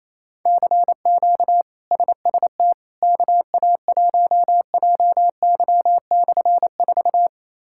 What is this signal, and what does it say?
Morse code, heard as a keyed tone, about 28 wpm, 700 Hz. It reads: CQ SST KA1JY/4
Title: K1USN SST QSO Practice - 28wpm